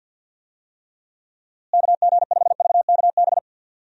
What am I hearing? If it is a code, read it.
Morse code, heard as a keyed tone, about 49 wpm, 700 Hz. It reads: KG5VKB
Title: USA Call Signs - 6 Characters - Repeated 3x - 50wpm